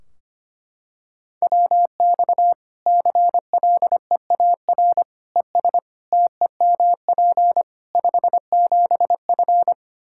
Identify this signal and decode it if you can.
Morse code, heard as a keyed tone, about 25 wpm, 700 Hz. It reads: WX CLEAR ES TEMP 57F
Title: Practice QSO - Exchange 2 - Info - 25wpm